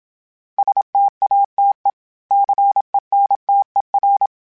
Morse code, heard as a keyed tone, about 27 wpm, 800 Hz. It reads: STATE CENTER
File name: fromSMs / FromSMs mp3s / Cities@25